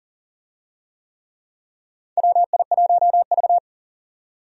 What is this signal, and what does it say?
Morse code, heard as a keyed tone, about 40 wpm, 700 Hz. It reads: WI1V
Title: USA Call Signs - 4 Characters - 40wpm